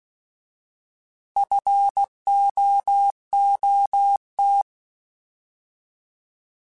Morse code, heard as a keyed tone, about 16 wpm, 800 Hz. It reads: FOOT
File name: Alphabetical / Alphabetical mp3s / Fs @15